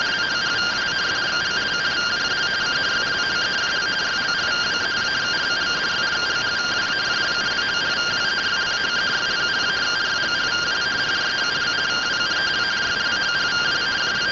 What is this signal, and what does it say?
Two-tone teleprinter signal, about 170 Hz shift.